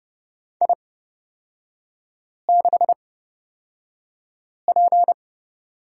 Morse code, heard as a keyed tone, about 29 wpm, spaced as for 5 wpm, 700 Hz. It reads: I6P
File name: Lesson 36e - BT - Three Characters 30wpm